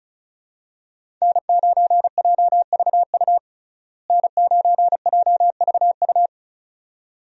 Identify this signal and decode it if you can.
Morse code, heard as a keyed tone, about 34 wpm, 700 Hz. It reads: N9JVU N9JVU
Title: USA Call Signs - 5 Characters - Repeated 2x - 35wpm